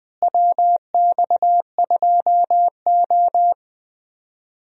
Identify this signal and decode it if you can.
Morse code, heard as a keyed tone, about 20 wpm, 700 Hz. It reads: WX2O
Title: USA Call Signs - 4 Characters - Repeated 2x - 20wpm